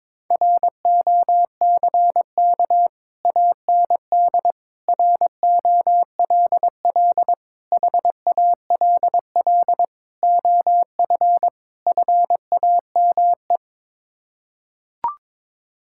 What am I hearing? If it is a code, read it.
Morse code, heard as a keyed tone, about 22 wpm, 700 Hz. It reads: ROCK AND ROLL HALL OF FAME
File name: practice / 22wpm / Sets of 6 Words - 22wpm